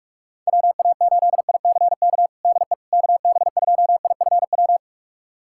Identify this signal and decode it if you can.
Morse code, heard as a keyed tone, about 46 wpm, 690 Hz. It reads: WA8ICK DE KB2IFW
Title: USA Callsign DE Callsign 45wpm